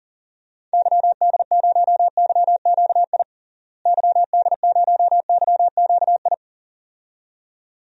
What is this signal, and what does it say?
Morse code, heard as a keyed tone, about 40 wpm, 700 Hz. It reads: YD0YQI YD0YQI
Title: DXCC-100 Call Signs - 6 Characters - Repeated 2x - 40wpm